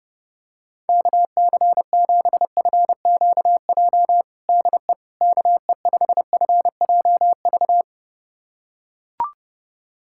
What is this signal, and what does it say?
Morse code, heard as a keyed tone, about 30 wpm, 700 Hz. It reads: KC7FQJ DE KE5FJV
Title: USA Callsign DE Callsign - 6 Characters - 30wpm